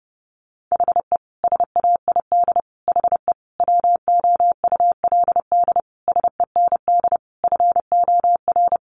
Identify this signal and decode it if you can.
Morse code, heard as a keyed tone, about 30 wpm, 700 Hz. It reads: HE SAID HE WOULD SEND FOR
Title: Sentences from Top 1500 Words 30wpm